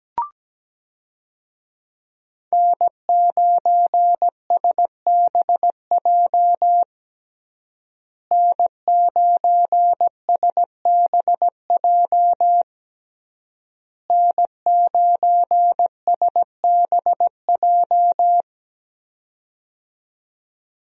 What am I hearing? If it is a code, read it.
Morse code, heard as a keyed tone, about 17 wpm, 700 Hz. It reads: N9SBJ N9SBJ N9SBJ
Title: USA Call Signs - 5 Characters - Repeated 3x - 17wpm